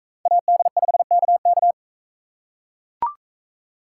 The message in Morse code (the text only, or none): AD5KK